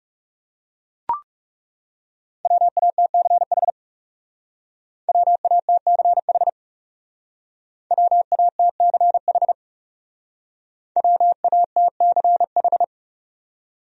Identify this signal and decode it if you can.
Morse code, heard as a keyed tone, about 37 wpm, 700 Hz. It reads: WATCH WATCH WATCH WATCH
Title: Speed-Racing Top 500 Words 30wpm